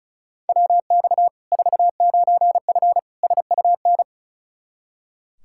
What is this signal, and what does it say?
Morse code, heard as a keyed tone, about 36 wpm, 700 Hz. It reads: WX 49F SUN
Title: Practice QSO - Exchange 2 - Info - 35wpm